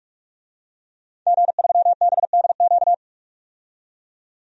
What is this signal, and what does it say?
Morse code, heard as a keyed tone, about 44 wpm, 700 Hz. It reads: G3BDQ
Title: DXCC-100 Call Signs - 5 Characters - 45wpm